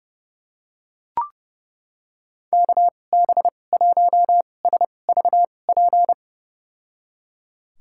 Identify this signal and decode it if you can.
Morse code, heard as a keyed tone, about 29 wpm, spaced as for 22 wpm, 700 Hz. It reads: KB1SVP
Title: USA Call Signs - 6 Characters - 2x Character Spacing - 30wpm